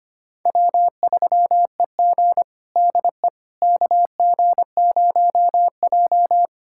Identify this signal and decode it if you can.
Morse code, heard as a keyed tone, about 25 wpm, 700 Hz. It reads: W3EG DE KG0J
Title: USA Callsign DE Callsign - 4 Characters - 25wpm